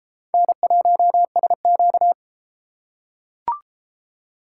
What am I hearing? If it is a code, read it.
Morse code, heard as a keyed tone, about 31 wpm, 700 Hz. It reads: N1SQ